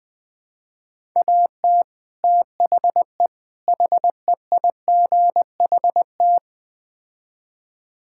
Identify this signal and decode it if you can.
Morse code, heard as a keyed tone, about 20 wpm, 700 Hz. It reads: AT THE HEIGHT
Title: Build A Phrase - 5-Word Phrases - 20wpm